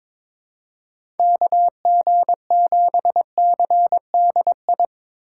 Morse code, heard as a keyed tone, about 22 wpm, 700 Hz. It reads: KG7CDI
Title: USA Call Signs - 6 Characters - 8x Character Spacing - 22wpm